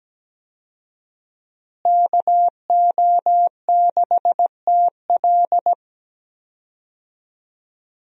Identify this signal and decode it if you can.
Morse code, heard as a keyed tone, about 17 wpm, 700 Hz. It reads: KO6TL
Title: USA Call Signs - 5 Characters - Repeated 2x - 17wpm